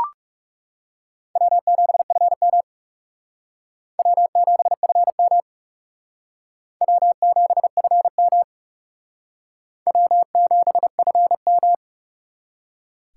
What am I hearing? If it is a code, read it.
Morse code, heard as a keyed tone, about 37 wpm, 700 Hz. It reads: W7FM W7FM W7FM W7FM
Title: Speed Racing - USA 1x2 Call Signs 30wpm